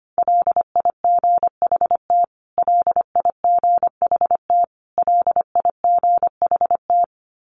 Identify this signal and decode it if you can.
Morse code, heard as a keyed tone, about 25 wpm, 700 Hz. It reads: LIGHT LIGHT LIGHT
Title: Top 200 Words Repeated 25wpm